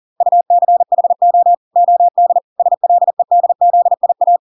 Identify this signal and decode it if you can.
Morse code, heard as a keyed tone, about 40 wpm, 700 Hz. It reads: UCHO OD SLEDZIA